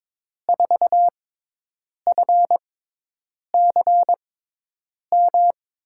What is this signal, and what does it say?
Morse code, heard as a keyed tone, about 22 wpm, spaced as for 8 wpm, 700 Hz. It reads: 4FCM
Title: USA Call Signs - 6x Character Spacing - 22wpm